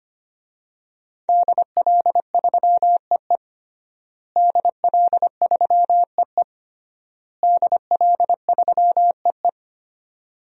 Morse code, heard as a keyed tone, about 25 wpm, 700 Hz. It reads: DL3EE DL3EE DL3EE
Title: DXCC-100 Call Signs - 5 Characters - Repeated 3x - 25wpm